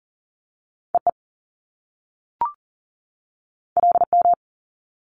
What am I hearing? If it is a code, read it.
Morse code, heard as a keyed tone, about 40 wpm, 700 Hz. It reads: EE LM
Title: Top Two Letter Combinations in Words 40wpm